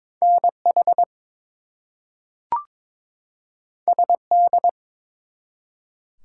Morse code, heard as a keyed tone, about 22 wpm, 700 Hz. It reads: NH SD
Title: US States and Canadian Provinces 22wpm